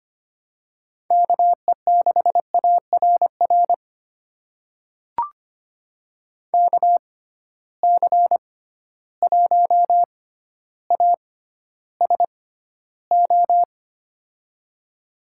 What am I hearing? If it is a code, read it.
Morse code, heard as a keyed tone, about 25 wpm, spaced as for 13 wpm, 700 Hz. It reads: KE6ARR KC1ASO
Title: USA Call Signs - 6 Characters - 6x Character Spacing - 25wpm